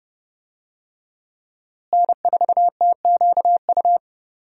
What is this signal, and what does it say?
Morse code, heard as a keyed tone, about 30 wpm, 700 Hz. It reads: N4TQU
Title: USA Call Signs - 5 Characters - Repeated 2x - 30wpm